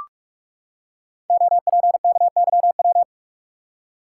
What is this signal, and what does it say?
Morse code, heard as a keyed tone, about 44 wpm, 700 Hz. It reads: OPKYW
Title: Code Groups - 5 Letter:Numbers - 45wpm